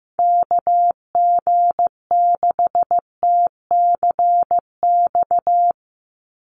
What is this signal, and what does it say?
Morse code, heard as a keyed tone, about 15 wpm, 700 Hz. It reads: KG6TCX
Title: USA Call Signs 15wpm